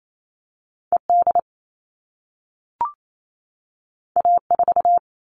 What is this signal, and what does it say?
Morse code, heard as a keyed tone, about 28 wpm, 700 Hz. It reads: ED A4
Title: All Cut Numbers - 2-Digit 28wpm